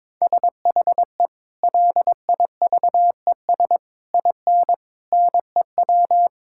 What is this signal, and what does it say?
Morse code, heard as a keyed tone, about 22 wpm, 700 Hz. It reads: SHE LIVES IN NEW
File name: practice / 22wpm / Sentences with 3-Word Phrases - 22wpm